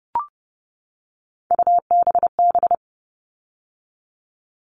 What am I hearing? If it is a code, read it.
Morse code, heard as a keyed tone, about 30 wpm, 700 Hz. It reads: UBB
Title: All Cut Numbers - 3-Digit 30wpm